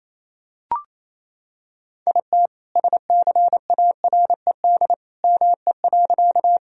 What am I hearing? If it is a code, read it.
Morse code, heard as a keyed tone, about 28 wpm, 700 Hz. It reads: IT SCARED ME.
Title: Sentences from Top 1000 Words - Review Part 4 - 28wpm